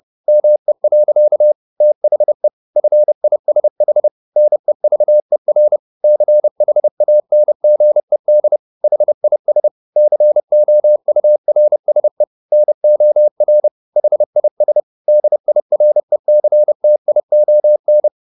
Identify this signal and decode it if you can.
Morse code, heard as a keyed tone, about 30 wpm, 600 Hz. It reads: ME. THE FISH NEVER CHANGED HIS COURSE NOR HIS DIRECTION
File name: the old man and the sea / the old man and the sea 1 30wpm